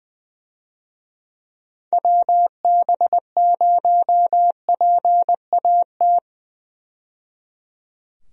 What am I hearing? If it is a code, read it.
Morse code, heard as a keyed tone, about 20 wpm, 700 Hz. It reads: WB0PAT
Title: USA Call Signs - 6 Characters - 20wpm